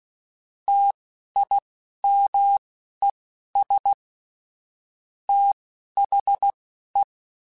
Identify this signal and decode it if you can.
Morse code, heard as a keyed tone, about 16 wpm, spaced as for 11 wpm, 800 Hz. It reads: TIMES THE
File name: Technical Subjects / Technical Subjects-mp3s / XL @10